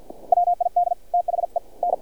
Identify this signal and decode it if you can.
Morse code, heard as a keyed tone, about 51 wpm, spaced as for 41 wpm, 700 Hz. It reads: WID THEN
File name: archived / CRSnet